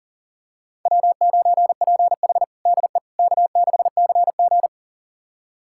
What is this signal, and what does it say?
Morse code, heard as a keyed tone, about 40 wpm, 700 Hz. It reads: W9PH DE K6CG
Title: USA Callsign DE Callsign - 4 Characters - 40wpm